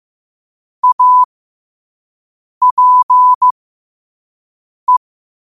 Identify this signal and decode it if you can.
Morse code, heard as a keyed tone, about 15 wpm, spaced as for 5 wpm, 1000 Hz. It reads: APE
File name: qotdfiles / 5wpm